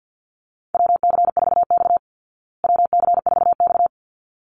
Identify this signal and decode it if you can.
Morse code, heard as a keyed tone, about 50 wpm, 700 Hz. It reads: WC4X WC4X
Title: Call Signs Repeated 3x - 50wpm